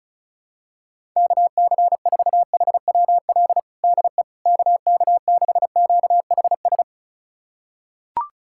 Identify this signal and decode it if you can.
Morse code, heard as a keyed tone, about 36 wpm, 700 Hz. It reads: KC4HWL DE KK6QHS
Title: USA Callsign DE Callsign 35wpm